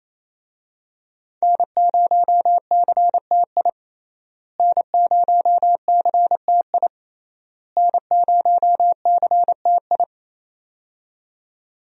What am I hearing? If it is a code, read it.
Morse code, heard as a keyed tone, about 28 wpm, 700 Hz. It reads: N0CTI N0CTI N0CTI
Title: USA Call Signs - 5 Characters - Repeated 3x - 28wpm